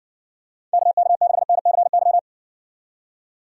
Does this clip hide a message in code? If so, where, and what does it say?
Yes: Morse code from 0.7 to 2.2 s, heard as a keyed tone, about 58 wpm, 700 Hz. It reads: KK6NCY